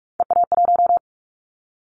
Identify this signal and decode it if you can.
Morse code, heard as a keyed tone, about 44 wpm, 700 Hz. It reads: EA1
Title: Callsign Prefixes for Top 25 DXCC - Repeated 3x - 45wpm